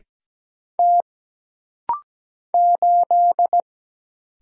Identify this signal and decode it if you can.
Morse code, heard as a keyed tone, about 17 wpm, 700 Hz. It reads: T 8
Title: Single Letter-Number Alt - Rapid-Fire 17wpm